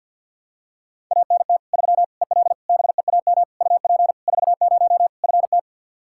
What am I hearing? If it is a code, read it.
Morse code, heard as a keyed tone, about 52 wpm, 700 Hz. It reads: ANT 3 EL BEAM UP 40 FT
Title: K1USN SST QSO Practice 50wpm